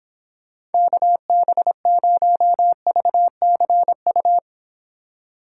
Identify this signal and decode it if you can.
Morse code, heard as a keyed tone, about 29 wpm, 700 Hz. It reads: KB0VCU